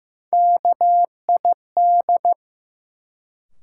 Morse code, heard as a keyed tone, about 15 wpm, 700 Hz. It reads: KID
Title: Sets of 5 Words - New Word Drill - 15wpm